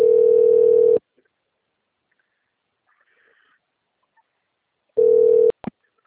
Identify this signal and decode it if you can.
Ringback tone (North America), 440 + 480 Hz.